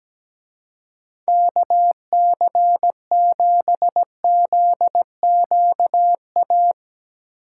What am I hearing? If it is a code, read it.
Morse code, heard as a keyed tone, about 19 wpm, 700 Hz. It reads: KC7ZQA
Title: Speed Racing - USA 2x3 Call Signs 17wpm